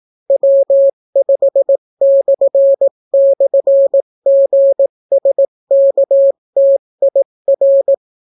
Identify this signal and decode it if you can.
Morse code, heard as a keyed tone, about 18 wpm, 550 Hz. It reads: W5//GSKTIR